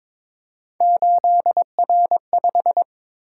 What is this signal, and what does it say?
Morse code, heard as a keyed tone, about 26 wpm, 700 Hz. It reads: 8R5
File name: Speed-Racing - Three Letter-Number Combinations - 22wpm